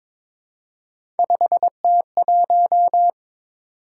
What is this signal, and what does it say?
Morse code, heard as a keyed tone, about 22 wpm, 700 Hz. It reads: 5T1
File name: practice / 22wpm / Three Letter-Number Combinations - 22wpm